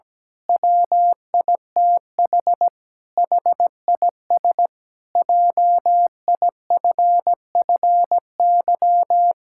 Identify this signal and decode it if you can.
Morse code, heard as a keyed tone, about 17 wpm, 700 Hz. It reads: WITH HIS JIFFY
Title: Pangram Challenge v2 - 17wpm